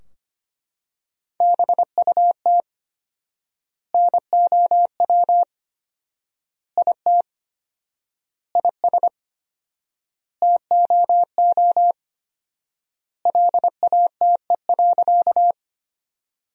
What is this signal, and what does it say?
Morse code, heard as a keyed tone, about 25 wpm, 700 Hz. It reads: BUT NOW IT IS TOO LATE.
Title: 4x Word-Spacing - Sentences from Top 300 Words - 25wpm